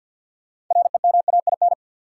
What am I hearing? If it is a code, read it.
Morse code, heard as a keyed tone, about 52 wpm, 700 Hz. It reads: REMAIN
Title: Six Letter Words - 50wpm